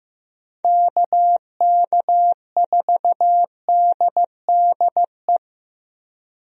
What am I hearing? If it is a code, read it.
Morse code, heard as a keyed tone, about 15 wpm, 700 Hz. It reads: KK4DDE
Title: USA Call Signs - 6 Characters - 15wpm